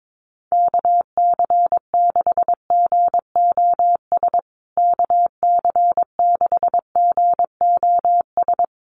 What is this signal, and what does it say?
Morse code, heard as a keyed tone, about 22 wpm, 700 Hz. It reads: KC6GOS KC6GOS
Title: USA Call Signs Repeated 3x 22wpm